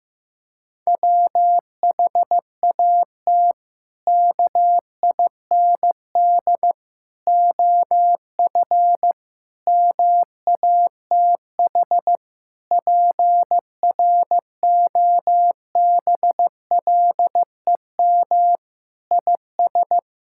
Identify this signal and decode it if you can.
Morse code, heard as a keyed tone, about 15 wpm, 700 Hz. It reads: WHAT KIND OF MATH PROBLEM IS
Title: Sentences from Top 1000 Words - Review Part 6 - 15wpm